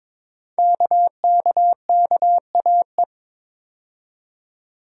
Morse code, heard as a keyed tone, about 22 wpm, 700 Hz. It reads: KKKAE
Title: Code Groups - 5 Letters - 22wpm